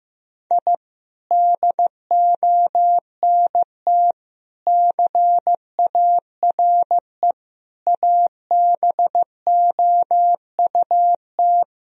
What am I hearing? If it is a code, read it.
Morse code, heard as a keyed tone, about 15 wpm, 700 Hz. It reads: I DONT CARE ABOUT
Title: Sentences from Top 1000 Words - Review Part 4 - 15wpm